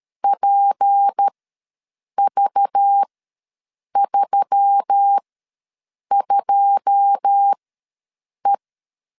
Morse code, heard as a keyed tone, about 13 wpm, spaced as for 7 wpm, 800 Hz. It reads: PV32E
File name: LEZIONI / 9 2 ws 12wpm 3406